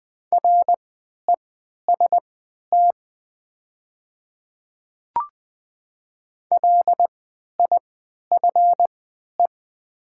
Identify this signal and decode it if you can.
Morse code, heard as a keyed tone, about 20 wpm, spaced as for 11 wpm, 690 Hz. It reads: REST LIFE
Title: 3x Character Spacing - Top 500 Words - 20wpm